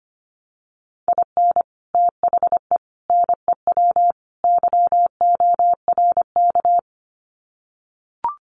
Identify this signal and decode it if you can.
Morse code, heard as a keyed tone, about 25 wpm, 700 Hz. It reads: IN THE NEW YORK
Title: Sets of 4 Words 25wpm